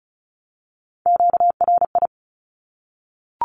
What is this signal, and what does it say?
Morse code, heard as a keyed tone, about 34 wpm, 700 Hz. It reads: QRI
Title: Q-Codes 35wpm